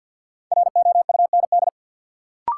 Morse code, heard as a keyed tone, about 46 wpm, 700 Hz. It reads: ROUND
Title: Sets of 5 Words Encore - New Word Drill - 50wpm